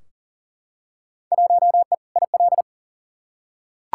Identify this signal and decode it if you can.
Morse code, heard as a keyed tone, about 40 wpm, 700 Hz. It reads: 1E IL
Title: Field Day Exchanges - 40wpm